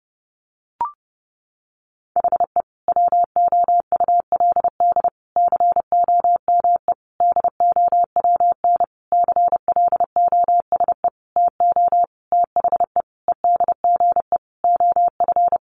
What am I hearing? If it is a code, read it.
Morse code, heard as a keyed tone, about 30 wpm, 700 Hz. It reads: HE WOULD COME DOWN CLOSE TO THE EDGE OF
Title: Sentences from Top 1300 Words 30wpm